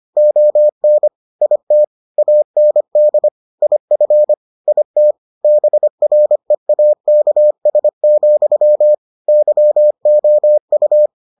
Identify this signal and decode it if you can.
Morse code, heard as a keyed tone, about 25 wpm, 600 Hz. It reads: ON IT AND IF IT BREAKS, YOU